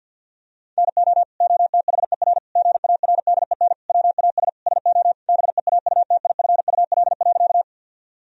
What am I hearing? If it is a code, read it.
Morse code, heard as a keyed tone, about 50 wpm, 700 Hz. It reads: NO OTHER GARDEN WAS SO BEAUTIFUL.
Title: Sentences from Top 1000 Words - Review Part 4 - 50wpm